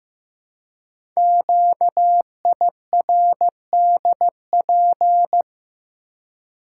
Morse code, heard as a keyed tone, about 15 wpm, 700 Hz. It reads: QIRDP
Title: Code Groups - 5 Letter:Numbers - 15wpm